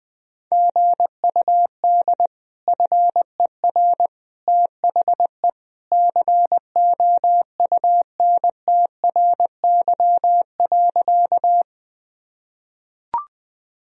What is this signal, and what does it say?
Morse code, heard as a keyed tone, about 20 wpm, 700 Hz. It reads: GUD FER THE COUNTRY.
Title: Sentences from Top 300 Words and Abbreviations 20wpm